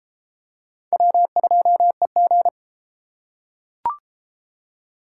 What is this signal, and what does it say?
Morse code, heard as a keyed tone, about 30 wpm, 700 Hz. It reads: W2EG